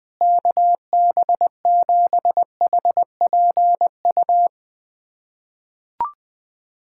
Morse code, heard as a keyed tone, about 20 wpm, 700 Hz. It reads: KB7HPU
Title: USA Callsign DE Callsign - 6 Characters - 20wpm